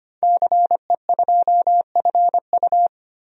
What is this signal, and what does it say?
Morse code, heard as a keyed tone, about 25 wpm, 700 Hz. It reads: CE2FU
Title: DXCC-100 Call Signs - 5 Characters - 25wpm